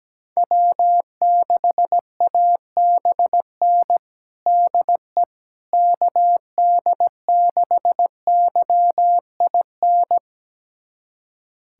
Morse code, heard as a keyed tone, about 17 wpm, 700 Hz. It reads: W6ABN DE KD6YIN
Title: USA Callsign DE Callsign 17wpm